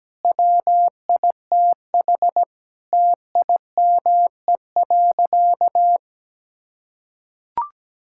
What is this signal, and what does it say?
Morse code, heard as a keyed tone, about 17 wpm, 700 Hz. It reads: WITH TIME.
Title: Sentences from Top 1000 Words - Review Part 5 - 17wpm